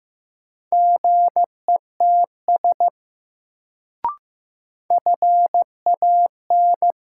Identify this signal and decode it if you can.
Morse code, heard as a keyed tone, about 15 wpm, 700 Hz. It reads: GETS FAN
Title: Sentences from Top 1300 Words - New Word Drill Alt - 15wpm